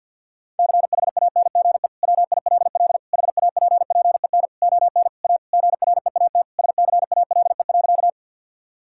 Morse code, heard as a keyed tone, about 50 wpm, 700 Hz. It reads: CHANGE WILL HAPPEN ON A GREAT SCALE.
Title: Sentences from Top 1000 Words - Review Part 5 - 50wpm